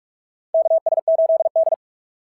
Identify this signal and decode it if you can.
Morse code, heard as a keyed tone, about 42 wpm, 690 Hz. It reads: KS8D
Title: SOTA QSO Practice 45wpm